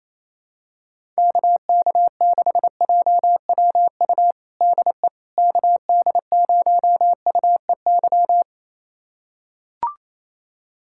Morse code, heard as a keyed tone, about 28 wpm, 700 Hz. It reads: KK6JWU DE KD0UEY
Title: USA Callsign DE Callsign 28wpm